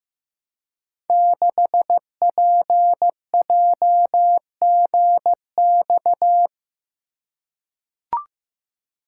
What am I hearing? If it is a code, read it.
Morse code, heard as a keyed tone, about 15 wpm, 700 Hz. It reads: 6PJGX
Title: Code Groups - 5 Letter:Numbers - 15wpm